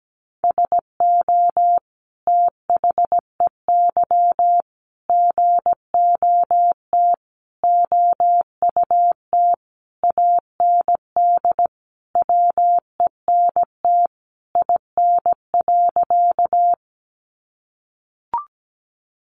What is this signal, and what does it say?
Morse code, heard as a keyed tone, about 17 wpm, 700 Hz. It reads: SO THEY GOT OUT AND WENT IN.
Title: Sentences from Top 300 Words 17wpm